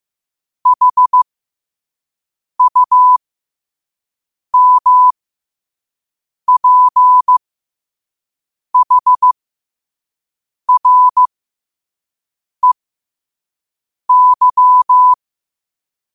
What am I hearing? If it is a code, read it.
Morse code, heard as a keyed tone, about 15 wpm, spaced as for 5 wpm, 1000 Hz. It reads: HUMPHREY
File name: qotdfiles / 5wpm